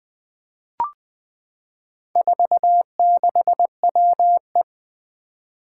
Morse code, heard as a keyed tone, about 20 wpm, 700 Hz. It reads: 46WE
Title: Four Challenging Letter-Number Combinations 20wpm